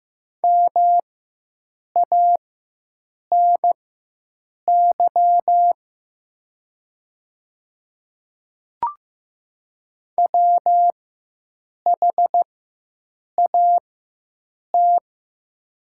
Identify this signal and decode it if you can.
Morse code, heard as a keyed tone, about 15 wpm, spaced as for 7 wpm, 700 Hz. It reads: MANY WHAT
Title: 4x Character Spacing - Top 100 Words - 15wpm